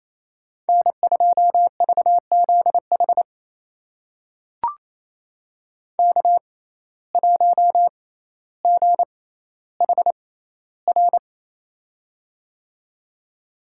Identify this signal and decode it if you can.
Morse code, heard as a keyed tone, about 28 wpm, spaced as for 14 wpm, 700 Hz. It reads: N2VZH K1GHR
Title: USA Call Signs - 5 Characters - 6x Character Spacing - 28wpm